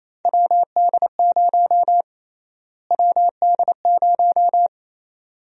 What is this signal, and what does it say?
Morse code, heard as a keyed tone, about 27 wpm, 700 Hz. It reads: WD0 WD0
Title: USA Call Sign Prefixes - Repeated 2x - 28wpm